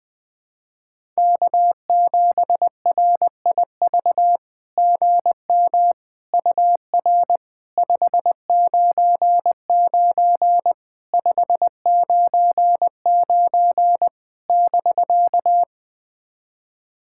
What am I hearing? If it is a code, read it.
Morse code, heard as a keyed tone, about 20 wpm, 700 Hz. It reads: K7RIV GM UR 599 599 <BK>
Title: SOTA QSO Practice 20wpm